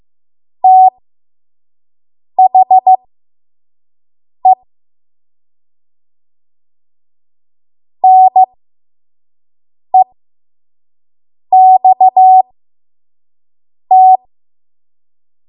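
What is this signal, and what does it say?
Morse code, heard as a keyed tone, about 15 wpm, spaced as for 5 wpm, 750 Hz. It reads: THE NEXT
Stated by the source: 5 WPM Morse code practice.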